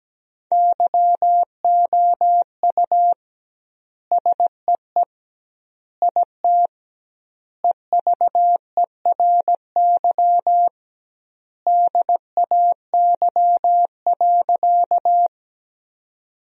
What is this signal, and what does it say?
Morse code, heard as a keyed tone, about 17 wpm, 700 Hz. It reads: YOU SEE IT EVERY DAY.
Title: 2x Word-Spacing - Sentences from Top 200 Words - 17wpm